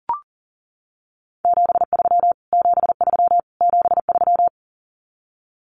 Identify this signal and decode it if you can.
Morse code, heard as a keyed tone, about 40 wpm, 700 Hz. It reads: 73 73 73
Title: Abbreviations Repeated 3x - 40wpm